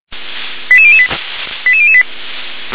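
Five-tone selective call (ZVEI), heard as quick runs of five tones, tones of about 70 ms. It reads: 88008 88008
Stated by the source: ZVEI five-tone selective call received by SDR.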